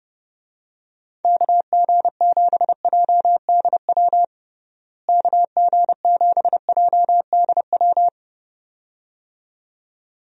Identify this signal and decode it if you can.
Morse code, heard as a keyed tone, about 30 wpm, 700 Hz. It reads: KG7JDW KG7JDW
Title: USA Call Signs - 6 Characters - Repeated 2x - 30wpm